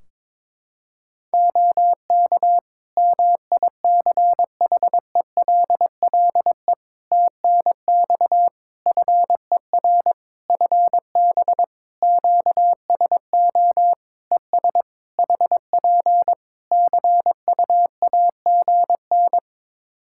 Morse code, heard as a keyed tone, about 23 wpm, 700 Hz. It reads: OK MICHELLE TNX FER FB QSO ES HP CUAGN
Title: Practice QSO - Exchange 4 - Ending - 22wpm